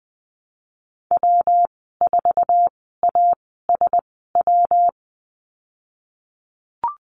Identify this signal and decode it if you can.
Morse code, heard as a keyed tone, about 20 wpm, spaced as for 15 wpm, 700 Hz. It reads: W4ASW
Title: 122 Lesson 20g - Number 3 - Call Signs 20wpm